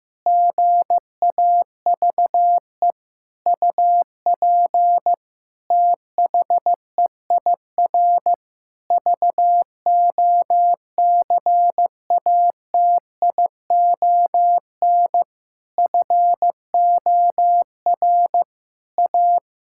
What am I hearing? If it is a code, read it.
Morse code, heard as a keyed tone, about 15 wpm, 700 Hz. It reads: GAVE UP THEIR VOCATION FOR A
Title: Pangram Challenge v2 - 15wpm